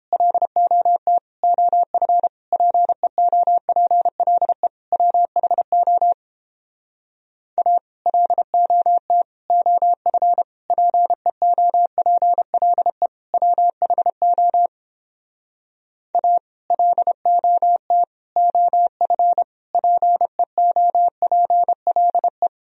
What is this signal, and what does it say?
Morse code, heard as a keyed tone, about 31 wpm, 700 Hz. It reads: LOT OF PEOPLE WHO A LOT OF PEOPLE WHO A LOT OF PEOPLE